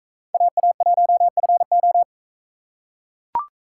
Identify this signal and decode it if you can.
Morse code, heard as a keyed tone, about 42 wpm, 700 Hz. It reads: AA1FO